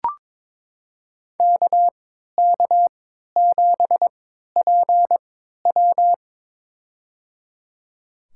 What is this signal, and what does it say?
Morse code, heard as a keyed tone, about 22 wpm, spaced as for 13 wpm, 700 Hz. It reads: KK7PW
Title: USA Call Signs - 5 Characters - 3x Character Spacing - 22wpm